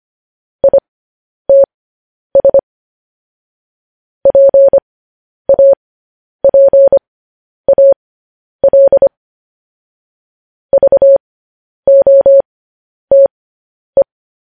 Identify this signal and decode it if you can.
Morse code, heard as a keyed tone, about 25 wpm, spaced as for 10 wpm, 550 Hz. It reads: ITS PAPAL VOTE
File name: podcast / News headlines-1x-25x10